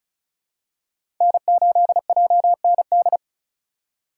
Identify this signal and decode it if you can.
Morse code, heard as a keyed tone, about 34 wpm, 700 Hz. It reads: N8JND
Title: USA Call Signs - 5 Characters - 4x Character Spacing - 35wpm